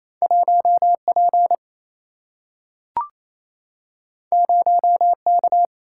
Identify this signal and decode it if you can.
Morse code, heard as a keyed tone, about 28 wpm, 700 Hz. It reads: 1 P 0 K
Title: Single-Letter Single-Number 28wpm